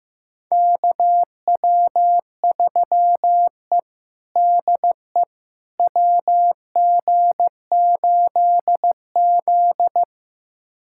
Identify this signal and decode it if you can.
Morse code, heard as a keyed tone, about 15 wpm, 700 Hz. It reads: KW3E DE WG8Z
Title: USA Callsign DE Callsign - 4 Characters - 15wpm